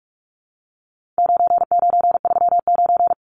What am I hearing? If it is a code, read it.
Morse code, heard as a keyed tone, about 44 wpm, 700 Hz. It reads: 9939
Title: Numbers - 4-Digit 45wpm